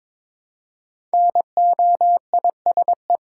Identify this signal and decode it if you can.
Morse code, heard as a keyed tone, about 22 wpm, 690 Hz. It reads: NOISE
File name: Difficult Words with Hints - 22wpm